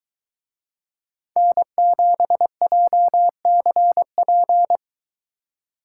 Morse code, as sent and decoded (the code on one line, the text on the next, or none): -. --... .--- -.-. .--.
N7JCP